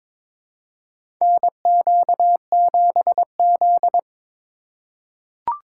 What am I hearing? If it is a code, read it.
Morse code, heard as a keyed tone, about 22 wpm, 700 Hz. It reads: NQ7Z
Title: USA Call Signs - 4 Characters - 4x Character Spacing - 22wpm